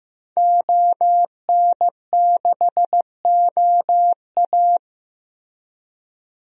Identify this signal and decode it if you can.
Morse code, heard as a keyed tone, about 16 wpm, 700 Hz. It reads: ON6OA
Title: DXCC-100 Call Signs - 5 Characters - Repeated 3x - 15wpm